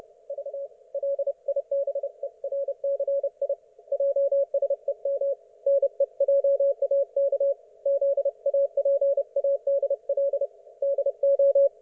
Morse code, heard as a keyed tone, about 30 wpm, 560 Hz. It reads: V LIBERCI JSEM NEJAK ZAPADL DO
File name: Archiv / CUC179 / ok1fhi cul179